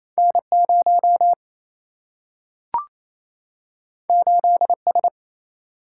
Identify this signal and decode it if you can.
Morse code, heard as a keyed tone, about 28 wpm, 700 Hz. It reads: N 0 8 S
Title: Single-Letter Single-Number 28wpm